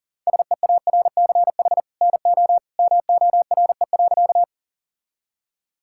Morse code, heard as a keyed tone, about 37 wpm, 700 Hz. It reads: SEARCH NO MORE.
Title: Sentences from Top 1000 Words - Review Part 5 - 40wpm